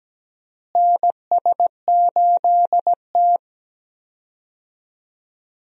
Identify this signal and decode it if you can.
Morse code, heard as a keyed tone, about 17 wpm, 700 Hz. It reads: NS8T
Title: USA Call Signs - 4 Characters - Repeated 2x - 17wpm